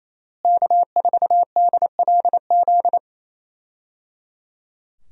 Morse code, heard as a keyed tone, about 28 wpm, 700 Hz. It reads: K4DLZ
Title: USA Call Signs - 5 Characters - Repeated 3x - 28wpm